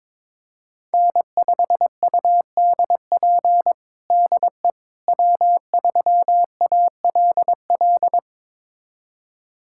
Morse code, heard as a keyed tone, about 21 wpm, 700 Hz. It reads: N5UDP DE W3ALL
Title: USA Callsign DE Callsign - 5 Characters - 22wpm